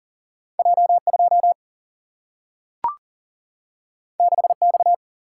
Morse code, heard as a keyed tone, about 40 wpm, 700 Hz. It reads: J2 6X
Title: Single-Letter Single-Number 40wpm